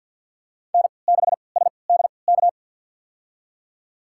Morse code, heard as a keyed tone, about 47 wpm, spaced as for 28 wpm, 700 Hz. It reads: N6SDK
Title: USA Call Signs - 5 Characters - 3x Character Spacing - 50wpm